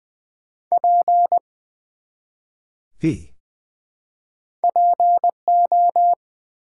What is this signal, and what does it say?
Morse code, heard as a keyed tone, about 20 wpm, 700 Hz. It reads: P PO
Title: Build A Word - 1 Letter At A Time - 4-Letter Words - 20wpm